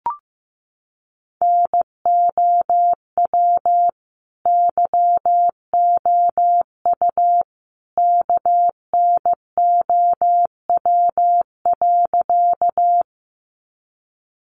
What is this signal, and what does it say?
Morse code, heard as a keyed tone, about 15 wpm, 700 Hz. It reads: NOW YOU KNOW.
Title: Sentences from Top 100 Words 15wpm